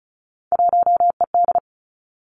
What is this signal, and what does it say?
Morse code, heard as a keyed tone, about 36 wpm, 700 Hz. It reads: 1ED
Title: Three Challenging Letter-Number Combinations 35wpm